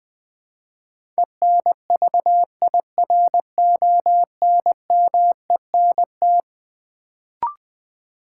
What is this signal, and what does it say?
Morse code, heard as a keyed tone, about 20 wpm, 700 Hz. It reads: ENVIRONMENT
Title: Build A Word - 4 Letters At A Time - 11-Letter Words - 20wpm